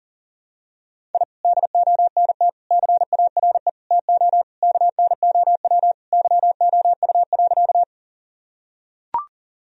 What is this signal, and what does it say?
Morse code, heard as a keyed tone, about 40 wpm, 700 Hz. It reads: I DONT CARE TO KNOW YOU.
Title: Sentences from Top 1000 Words - Review Part 1 - 40wpm